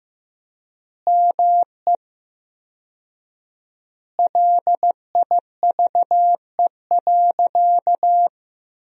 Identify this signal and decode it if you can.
Morse code, heard as a keyed tone, about 15 wpm, 700 Hz. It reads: ME LIVE.
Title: 4x Word-Spacing - Sentences from Top 300 Words - 15wpm